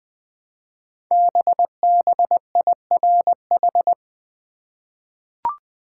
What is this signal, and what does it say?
Morse code, heard as a keyed tone, about 20 wpm, 700 Hz. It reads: BBIRH
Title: Code Groups - 5 Letters - 20wpm